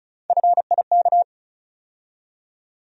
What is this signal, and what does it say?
Morse code, heard as a keyed tone, about 36 wpm, 700 Hz. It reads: FIK
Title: USA Call Sign Postfixes - 3 Letters 35wpm